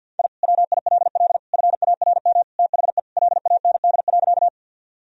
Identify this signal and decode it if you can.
Morse code, heard as a keyed tone, about 50 wpm, 700 Hz. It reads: I WILL FARM THE LAND.
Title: Sentences from Top 1000 Words - Review Part 2 - 50wpm